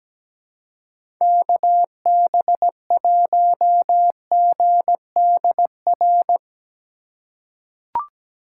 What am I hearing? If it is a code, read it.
Morse code, heard as a keyed tone, about 17 wpm, 700 Hz. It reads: KB1GDR
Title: USA Call Signs - 6 Characters - 4x Character Spacing - 17wpm